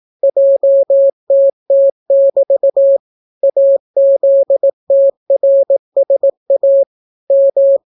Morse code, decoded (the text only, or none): JTT<BT> AZTRSA M